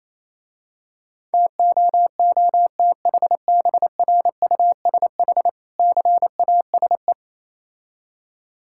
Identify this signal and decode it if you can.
Morse code, heard as a keyed tone, about 27 wpm, 700 Hz. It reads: TOOTHBRUSH CASE
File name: Easy Words - Speed-Racing with Hints - 22wpm